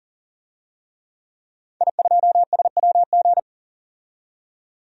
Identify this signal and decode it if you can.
Morse code, heard as a keyed tone, about 40 wpm, 700 Hz. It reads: I2SWG
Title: DXCC-100 Call Signs - 5 Characters - 40wpm